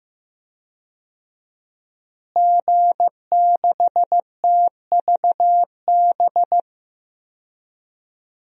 Morse code, heard as a keyed tone, about 15 wpm, 700 Hz. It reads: G6TVB
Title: DXCC-100 Call Signs - 5 Characters - Repeated 3x - 15wpm